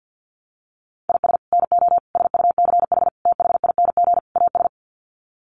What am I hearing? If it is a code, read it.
Morse code, heard as a keyed tone, about 50 wpm, 700 Hz. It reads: IS NO SUCH THING AS
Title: Sets of 5 Words Encore 50wpm